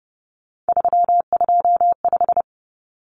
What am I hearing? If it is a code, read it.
Morse code, heard as a keyed tone, about 30 wpm, 700 Hz. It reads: 325
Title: Numbers - 3-Digit 30wpm